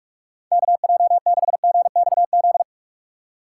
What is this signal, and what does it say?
Morse code, heard as a keyed tone, about 43 wpm, 700 Hz. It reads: KJ6GXZ
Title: USA Call Signs - 6 Characters - 8x Character Spacing - 45wpm